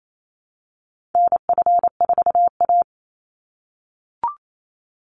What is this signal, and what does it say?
Morse code, heard as a keyed tone, about 28 wpm, 700 Hz. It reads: NF4A
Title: Call Signs - 28wpm